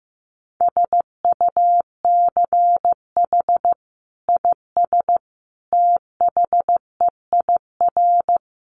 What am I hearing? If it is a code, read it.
Morse code, heard as a keyed tone, about 15 wpm, 700 Hz. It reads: SUCH IS THEIR
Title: Sentences from Top 900 Words 15wpm